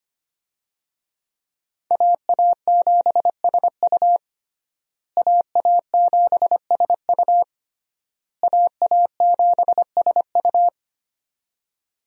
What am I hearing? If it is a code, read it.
Morse code, heard as a keyed tone, about 25 wpm, 700 Hz. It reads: AA7SU AA7SU AA7SU
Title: USA Call Signs - 5 Characters - Repeated 3x - 25wpm